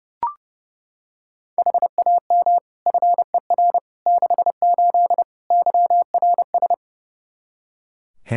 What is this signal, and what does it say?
Morse code, heard as a keyed tone, about 31 wpm, 700 Hz. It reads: HAM FER 68 YRS
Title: K1USN SST QSO Practice 30wpm